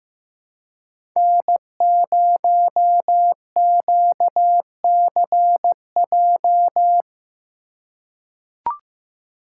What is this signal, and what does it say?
Morse code, heard as a keyed tone, about 15 wpm, 700 Hz. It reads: N0QCJ
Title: USA Call Signs - 5 Characters - 4x Character Spacing - 15wpm